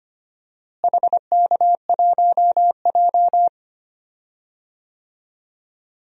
Morse code, heard as a keyed tone, about 25 wpm, 700 Hz. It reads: HK1J
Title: DXCC-100 Call Signs - 4 Characters - Repeated 2x - 25wpm